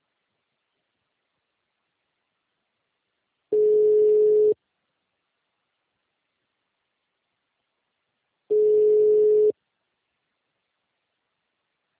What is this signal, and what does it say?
Ringback tone (Europe, 425 Hz).